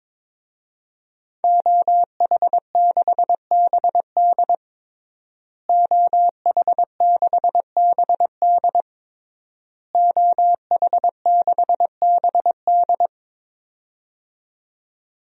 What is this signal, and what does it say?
Morse code, heard as a keyed tone, about 22 wpm, 700 Hz. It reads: OH6BD OH6BD OH6BD
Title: DXCC-100 Call Signs - 5 Characters - Repeated 3x - 22wpm